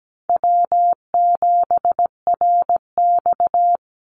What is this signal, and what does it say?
Morse code, heard as a keyed tone, about 17 wpm, 700 Hz. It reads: W7RX
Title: Call Signs Repeated 2x - 17wpm